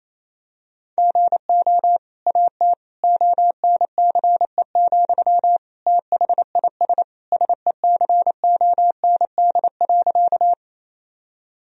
Morse code, heard as a keyed tone, about 28 wpm, 700 Hz. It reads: GO AT ONCE, THIS SECOND.
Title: Sentences from Top 1000 Words - Review Part 1 - 28wpm